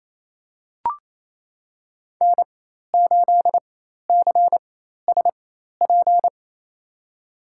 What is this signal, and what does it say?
Morse code, heard as a keyed tone, about 28 wpm, spaced as for 13 wpm, 700 Hz. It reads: N8CSP
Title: USA Call Signs - 5 Characters - 4x Character Spacing - 28wpm